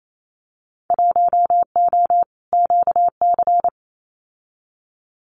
Morse code, heard as a keyed tone, about 28 wpm, 700 Hz. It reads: 1O QC
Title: Winter Field Day Exchanges 28wpm